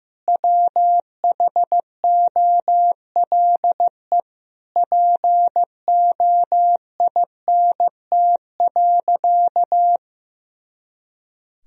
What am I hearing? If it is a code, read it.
Morse code, heard as a keyed tone, about 15 wpm, 700 Hz. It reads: WHOLE POINT.
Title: Sentences from Top 400 Words and Abbreviations 15wpm